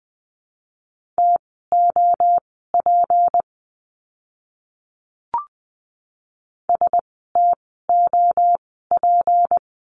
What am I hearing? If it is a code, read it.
Morse code, heard as a keyed tone, about 20 wpm, spaced as for 14 wpm, 700 Hz. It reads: TOP STOP
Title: 153 Lesson 25f - Letter p - Words 20wpm